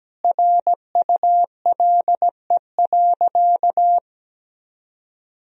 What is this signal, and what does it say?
Morse code, heard as a keyed tone, about 17 wpm, 700 Hz. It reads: RULE.
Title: Sentences from Top 1000 Words - Review Part 2 - 17wpm